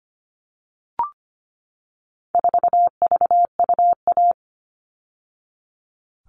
Morse code, heard as a keyed tone, about 25 wpm, 700 Hz. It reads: 4VUA
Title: All Cut Numbers - 4-Digit 25wpm